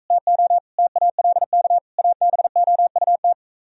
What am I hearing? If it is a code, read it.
Morse code, heard as a keyed tone, about 45 wpm, 700 Hz. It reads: TO TALK ABOUT